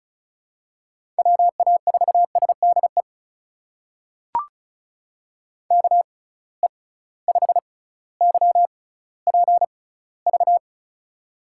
Morse code, heard as a keyed tone, about 36 wpm, spaced as for 18 wpm, 700 Hz. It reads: WA4SDE KE5YPV
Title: USA Call Signs - 6 Characters - 6x Character Spacing - 35wpm